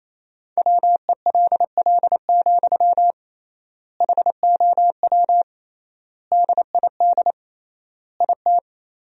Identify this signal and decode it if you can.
Morse code, heard as a keyed tone, about 28 wpm, 700 Hz. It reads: WELL, HOW DID IT
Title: 3x Word-Spacing - Sentences from Top 200 Words - 28wpm